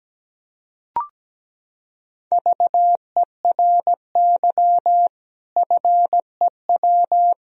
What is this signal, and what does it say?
Morse code, heard as a keyed tone, about 17 wpm, 700 Hz. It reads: VERY FEW
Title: Sentences from Top 1000 Words - Review Part 1 - 17wpm